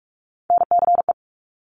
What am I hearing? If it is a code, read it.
Morse code, heard as a keyed tone, about 46 wpm, 700 Hz. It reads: NCE
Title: Top Three Letter Combinations in Words 45wpm